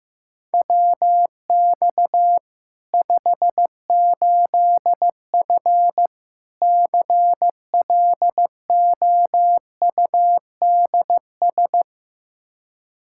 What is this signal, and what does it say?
Morse code, heard as a keyed tone, about 15 wpm, 700 Hz. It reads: WX 58F CLOUDS
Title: Practice QSO - Exchange 2 - Info - 15wpm